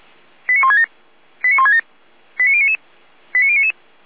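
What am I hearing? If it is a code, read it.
Five-tone selective call (ZVEI), heard as quick runs of five tones, tones of about 80 ms. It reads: 89167 89167 89099 89099
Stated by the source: Five-tone ZVEI selective call 89001, received by SDR, from the Feuerwehr Orlamünde fire brigade.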